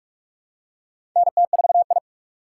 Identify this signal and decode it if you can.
Morse code, heard as a keyed tone, about 43 wpm, 700 Hz. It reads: NT4I
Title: USA Call Signs - 4 Characters - 2x Character Spacing - 45wpm